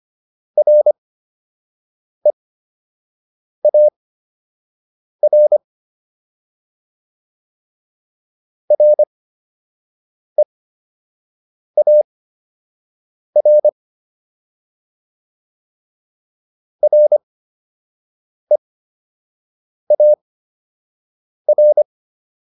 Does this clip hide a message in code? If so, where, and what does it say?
Yes: Morse code from 0.6 to 21.8 s, heard as a keyed tone, about 25 wpm, spaced as for 6 wpm, 600 Hz. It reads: REAR REAR REAR